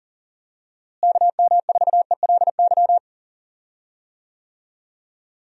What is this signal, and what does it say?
Morse code, heard as a keyed tone, about 40 wpm, 700 Hz. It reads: KM4ELY
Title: USA Call Signs - 6 Characters - Repeated 2x - 40wpm